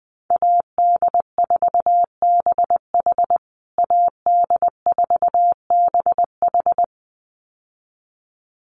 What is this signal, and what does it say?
Morse code, heard as a keyed tone, about 20 wpm, 700 Hz. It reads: AD4BH AD4BH
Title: USA Call Signs Repeated 2x 20wpm